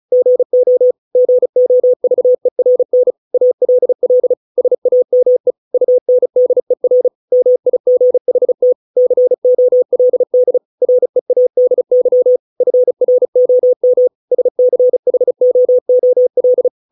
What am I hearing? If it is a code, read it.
Morse code, heard as a keyed tone, about 34 wpm, 500 Hz. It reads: GO GOVERN ALL SAME UNDER MIGHT COLD READY FROM SCHOOL
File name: 632 words / 632 Words 35WPM